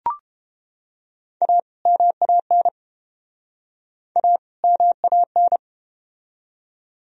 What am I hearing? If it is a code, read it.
Morse code, heard as a keyed tone, about 31 wpm, 700 Hz. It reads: A MAN A MAN